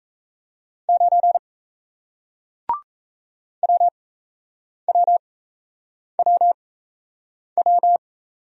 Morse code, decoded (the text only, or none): 9 WWWW